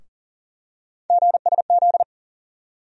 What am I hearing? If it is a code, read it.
Morse code, heard as a keyed tone, about 40 wpm, 700 Hz. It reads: GSZ
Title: USA Call Sign Postfixes - 3 Letters 40wpm